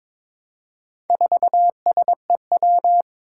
Morse code, heard as a keyed tone, about 22 wpm, 700 Hz. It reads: 4SEW
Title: Four Challenging Letter-Number Combinations 22wpm